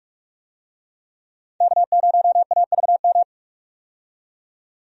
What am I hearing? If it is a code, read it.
Morse code, heard as a keyed tone, about 46 wpm, 700 Hz. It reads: K0AVM
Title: USA Call Signs - 5 Characters - 45wpm